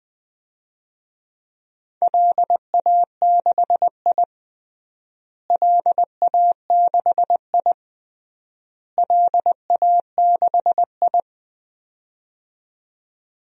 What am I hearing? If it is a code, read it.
Morse code, heard as a keyed tone, about 21 wpm, 700 Hz. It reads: LA6I LA6I LA6I
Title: DXCC-100 Call Signs - 4 Characters - Repeated 3x - 20wpm